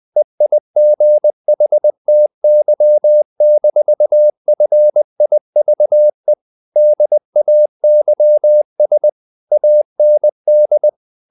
Morse code, heard as a keyed tone, about 20 wpm, 600 Hz. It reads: EIGHTY-FIVE DAYS AND
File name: the old man and the sea / the old man and the sea part 1 20wpm